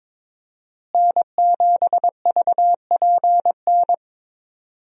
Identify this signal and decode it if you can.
Morse code, heard as a keyed tone, about 22 wpm, 700 Hz. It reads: N7VPN
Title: USA Call Signs - 5 Characters - Repeated 2x - 22wpm